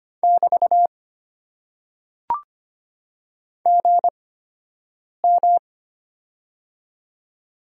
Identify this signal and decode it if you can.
Morse code, heard as a keyed tone, about 25 wpm, spaced as for 7 wpm, 700 Hz. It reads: <BT> GM
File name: Top 90 QSO Elements - 8x Character Spacing - 25wpm